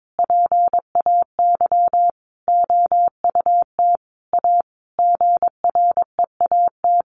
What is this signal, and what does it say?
Morse code, heard as a keyed tone, about 22 wpm, 700 Hz. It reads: PAY OUT A GREAT
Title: Sentences from Top 1500 Words 22wpm